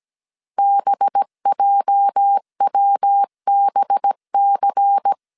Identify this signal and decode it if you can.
Morse code, heard as a keyed tone, about 17 wpm, 800 Hz. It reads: BJWBC
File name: CW / DET / 1 9 1500C S A 16wpm 2828